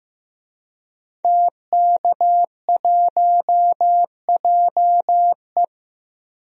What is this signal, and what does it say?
Morse code, heard as a keyed tone, about 15 wpm, 700 Hz. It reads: TK1JE
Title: DXCC-100 Call Signs - 5 Characters - 15wpm